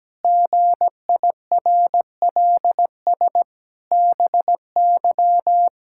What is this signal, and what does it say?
Morse code, heard as a keyed tone, about 17 wpm, 700 Hz. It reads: GIRLS BY
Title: Pangram Challenge v2 - 17wpm